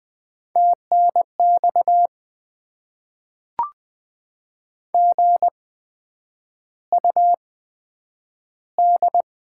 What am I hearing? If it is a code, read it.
Morse code, heard as a keyed tone, about 20 wpm, spaced as for 9 wpm, 700 Hz. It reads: TNX GUD
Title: Top 20 QSO Elements - 8x Character Spacing - 20wpm